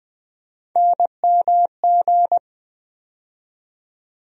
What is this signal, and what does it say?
Morse code, heard as a keyed tone, about 20 wpm, 700 Hz. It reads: NMG
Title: USA Call Sign Postfixes - 3 Letters 20wpm